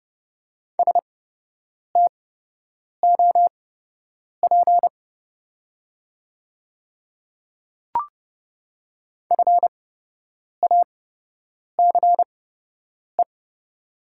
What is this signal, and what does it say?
Morse code, heard as a keyed tone, about 30 wpm, spaced as for 8 wpm, 700 Hz. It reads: STOP FACE
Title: 8x Character Spacing - Top 500 Words - 30wpm